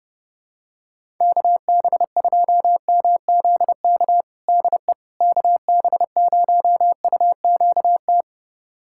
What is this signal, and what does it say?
Morse code, heard as a keyed tone, about 32 wpm, 700 Hz. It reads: KB2MZK DE KB0UQT
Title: USA Callsign DE Callsign 30wpm